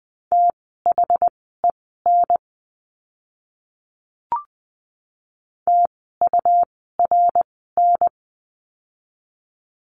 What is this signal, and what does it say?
Morse code, heard as a keyed tone, about 20 wpm, spaced as for 14 wpm, 700 Hz. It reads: THEN TURN
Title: 244 Lesson 39f - Letter z - Words 20wpm